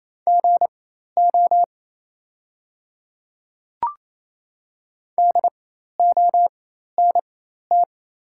Morse code, heard as a keyed tone, about 28 wpm, spaced as for 13 wpm, 700 Hz. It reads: GO DONT
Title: 4x Character Spacing - Top 500 Words - 28wpm